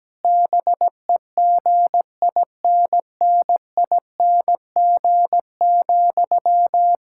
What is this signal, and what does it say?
Morse code, heard as a keyed tone, about 17 wpm, 700 Hz. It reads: BEGINNING,
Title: Sentences from Top 1000 Words - Review Part 6 - 17wpm